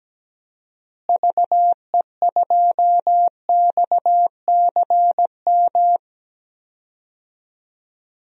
Morse code, heard as a keyed tone, about 17 wpm, 700 Hz. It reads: VE2XCM
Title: DXCC-100 Call Signs - 6 Characters - Repeated 2x - 17wpm